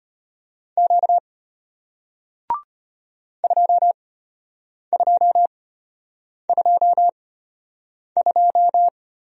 Morse code, heard as a keyed tone, about 33 wpm, spaced as for 8 wpm, 700 Hz. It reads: Q 2222